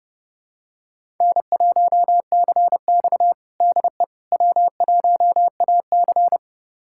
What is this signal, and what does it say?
Morse code, heard as a keyed tone, about 30 wpm, 700 Hz. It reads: N1CX DE W1AC
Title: USA Callsign DE Callsign - 4 Characters - 30wpm